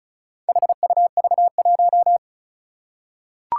Morse code, heard as a keyed tone, about 36 wpm, 690 Hz. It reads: HUV1
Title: Four Challenging Letter-Number Combinations 35wpm